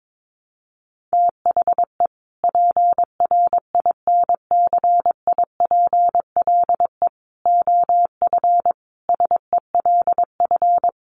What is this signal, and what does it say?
Morse code, heard as a keyed tone, about 22 wpm, 700 Hz. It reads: THE PRINCIPLE OF SELF
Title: Sentences from Top 1400 Words 22wpm